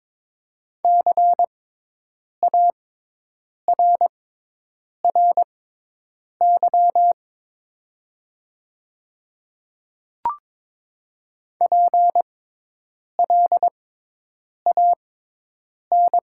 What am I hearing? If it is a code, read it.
Morse code, heard as a keyed tone, about 22 wpm, spaced as for 8 wpm, 700 Hz. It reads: CARRY PLAN
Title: 6x Character Spacing - Top 500 Words - 22wpm